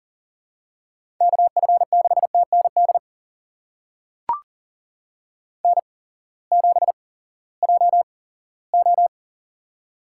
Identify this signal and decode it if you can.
Morse code, heard as a keyed tone, about 40 wpm, spaced as for 20 wpm, 700 Hz. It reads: KF6TND N7JO
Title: USA Call Signs - 8x Character Spacing - 40wpm